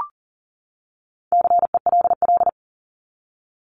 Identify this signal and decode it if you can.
Morse code, heard as a keyed tone, about 40 wpm, 700 Hz. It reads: CELL
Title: ES:S Suffix - Top 100 Words with Base Words 40wpm